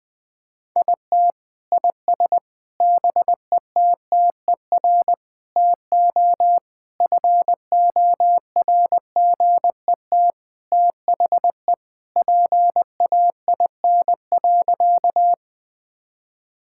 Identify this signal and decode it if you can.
Morse code, heard as a keyed tone, about 20 wpm, 700 Hz. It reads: IT IS BETTER TO FORGET THE PAIN.
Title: Sentences from Top 1600 Words 20wpm